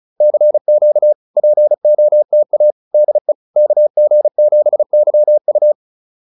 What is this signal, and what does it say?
Morse code, heard as a keyed tone, about 36 wpm, 590 Hz. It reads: CQ POTA DE KG7YU